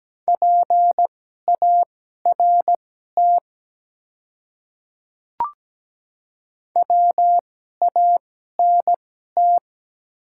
Morse code, heard as a keyed tone, about 17 wpm, spaced as for 12 wpm, 700 Hz. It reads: PART WANT
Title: 2x Character Spacing - Top 200 Words - 17wpm